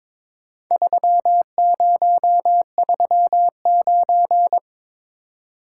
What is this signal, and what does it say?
Morse code, heard as a keyed tone, about 22 wpm, 700 Hz. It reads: 3039
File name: practice / 22wpm / Four Letter-Number Combinations - 22wpm